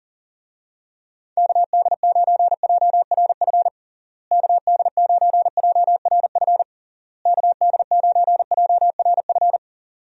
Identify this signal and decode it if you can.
Morse code, heard as a keyed tone, about 40 wpm, 700 Hz. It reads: KD9JRF KD9JRF KD9JRF
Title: USA Call Signs - 6 Characters - Repeated 3x - 40wpm